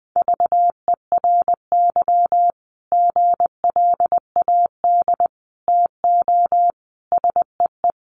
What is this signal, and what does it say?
Morse code, heard as a keyed tone, about 19 wpm, 700 Hz. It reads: VERY GLAD TO SEE
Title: Sentences from Top 800 Words 20wpm